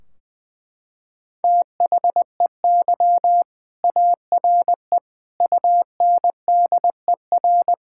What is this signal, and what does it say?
Morse code, heard as a keyed tone, about 20 wpm, 700 Hz. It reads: THEY ARE UNDER
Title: Sentences from Top 1600 Words 20wpm